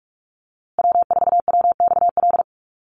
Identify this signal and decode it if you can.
Morse code, heard as a keyed tone, about 44 wpm, 700 Hz. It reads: W4WXL
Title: USA Call Signs Repeated 2x 45wpm